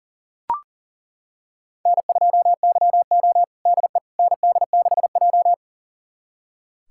Morse code, heard as a keyed tone, about 40 wpm, 700 Hz. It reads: N2YO DE ND6J
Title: USA Callsign DE Callsign - 4 Characters - 40wpm